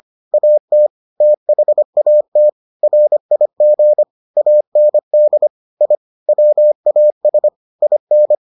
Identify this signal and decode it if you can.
Morse code, heard as a keyed tone, about 25 wpm, 600 Hz. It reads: AT THAT RIG AND I WAS IN